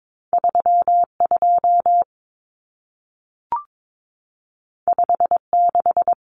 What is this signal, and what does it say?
Morse code, heard as a keyed tone, about 22 wpm, 700 Hz. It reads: 32 56
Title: Numbers - 2-Digit 22wpm